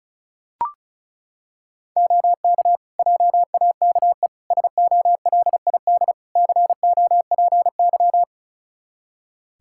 Morse code, heard as a keyed tone, about 36 wpm, 700 Hz. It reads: OK JAKE SOLID COPY
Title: Practice QSO - Exchange 3 - More Info - 35wpm